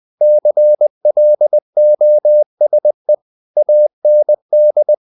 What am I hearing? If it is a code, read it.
Morse code, heard as a keyed tone, about 20 wpm, 600 Hz. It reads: CLOSE AND
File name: CW / Practice / EnglishSayings20